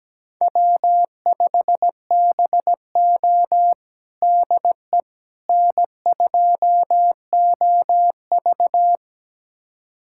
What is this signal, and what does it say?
Morse code, heard as a keyed tone, about 17 wpm, 700 Hz. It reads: W5BO DE N2OV
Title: USA Callsign DE Callsign - 4 Characters - 17wpm